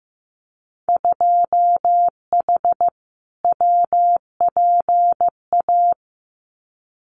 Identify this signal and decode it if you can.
Morse code, heard as a keyed tone, about 15 wpm, 700 Hz. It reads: 2H WPA
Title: Winter Field Day Exchanges 15wpm